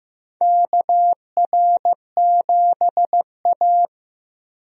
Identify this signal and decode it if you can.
Morse code, heard as a keyed tone, about 15 wpm, 700 Hz. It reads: KR7A
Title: USA Call Signs - 4 Characters - Repeated 2x - 15wpm